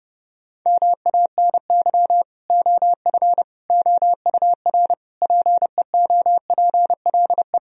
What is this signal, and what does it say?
Morse code, heard as a keyed tone, about 32 wpm, 700 Hz. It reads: MANY OF OUR PEOPLE
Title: Sentences from Top 1000 Words - Review Part 3 - 30wpm